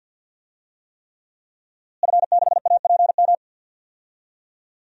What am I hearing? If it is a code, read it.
Morse code, heard as a keyed tone, about 51 wpm, 700 Hz. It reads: F6APM
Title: DXCC-100 Call Signs - 5 Characters - Repeated 3x - 50wpm